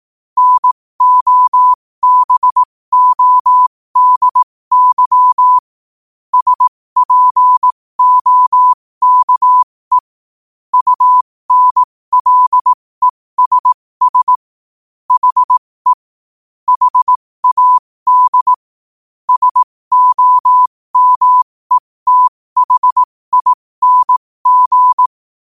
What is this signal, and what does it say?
Morse code, heard as a keyed tone, about 18 wpm, spaced as for 15 wpm, 1000 Hz. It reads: NOBODY SPOKE UNLESS HE HAD SOMETHING
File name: qotdfiles / 15wpm